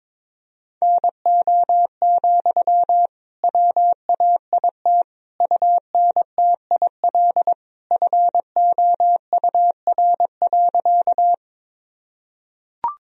Morse code, heard as a keyed tone, about 22 wpm, 700 Hz. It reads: NO, WAIT UNTIL FOUR.
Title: Sentences from Top 1000 Words - Review Part 1 - 22wpm